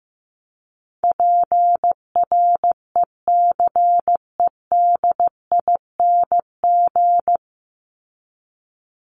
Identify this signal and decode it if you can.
Morse code, heard as a keyed tone, about 15 wpm, 700 Hz. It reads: PRECEDING
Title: ING Suffix - Top 100 Words with Base Words 15wpm